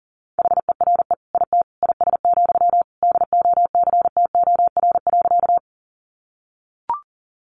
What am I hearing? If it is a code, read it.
Morse code, heard as a keyed tone, about 40 wpm, 700 Hz. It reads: HERE IT IS, DOCTOR.
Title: Sentences from Top 1500 Words 40wpm